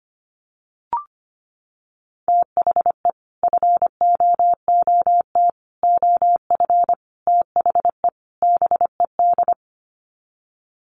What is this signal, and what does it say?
Morse code, heard as a keyed tone, about 25 wpm, 700 Hz. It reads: THE FOOT OF THE BED
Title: Sets of 5 Words Encore2 25wpm